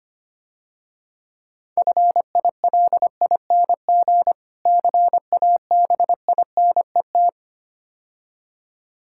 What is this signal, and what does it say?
Morse code, heard as a keyed tone, about 26 wpm, 700 Hz. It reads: FILING CABINET
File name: Easy Words - Speed-Racing with Hints - 25wpm